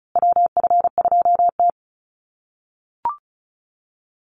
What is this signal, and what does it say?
Morse code, heard as a keyed tone, about 34 wpm, 700 Hz. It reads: WF2T
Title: USA Call Signs Repeated 2x 35wpm